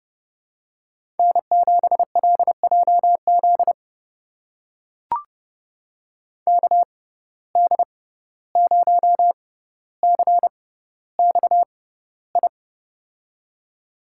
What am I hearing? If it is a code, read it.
Morse code, heard as a keyed tone, about 30 wpm, spaced as for 15 wpm, 700 Hz. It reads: N7LJZ KD0CXI
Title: USA Call Signs - 6x Character Spacing - 30wpm